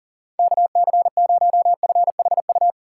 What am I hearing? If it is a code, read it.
Morse code, heard as a keyed tone, about 40 wpm, 700 Hz. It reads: KC0FHU
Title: USA Call Signs - 6 Characters - 3x Character Spacing - 40wpm